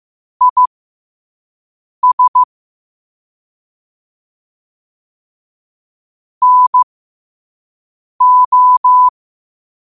Morse code, heard as a keyed tone, about 15 wpm, spaced as for 5 wpm, 1000 Hz. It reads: IS NO
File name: qotdfiles / 5wpm